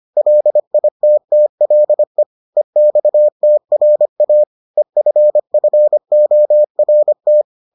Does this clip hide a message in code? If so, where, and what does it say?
Yes: Morse code from 0.2 to 7.4 s, heard as a keyed tone, about 25 wpm, 600 Hz. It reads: LITTLE EXTRA EFFORT